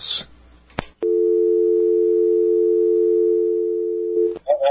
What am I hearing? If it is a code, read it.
Dial tone (North America), 350 + 440 Hz.